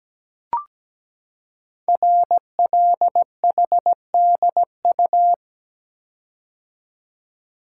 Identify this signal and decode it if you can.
Morse code, heard as a keyed tone, about 17 wpm, 700 Hz. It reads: RLHDU
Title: Code Groups - 5 Letters - 17wpm